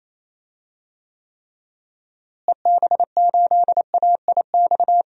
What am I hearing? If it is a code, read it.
Morse code, heard as a keyed tone, about 28 wpm, 700 Hz. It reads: EB8AIX
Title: DXCC-100 Call Signs - 6 Characters - Repeated 3x - 28wpm